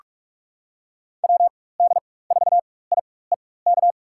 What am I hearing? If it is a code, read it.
Morse code, heard as a keyed tone, about 45 wpm, spaced as for 21 wpm, 700 Hz. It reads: WD4IEK
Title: USA Call Signs - 6 Characters - 4x Character Spacing - 45wpm